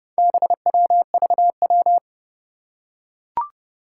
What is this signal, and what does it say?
Morse code, heard as a keyed tone, about 28 wpm, 700 Hz. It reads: BWVW
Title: Four Challenging Letter-Number Combinations 30wpm